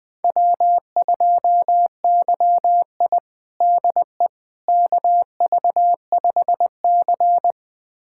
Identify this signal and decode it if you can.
Morse code, heard as a keyed tone, about 20 wpm, 700 Hz. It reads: W2YI DE KV5C
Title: USA Callsign DE Callsign - 4 Characters - 20wpm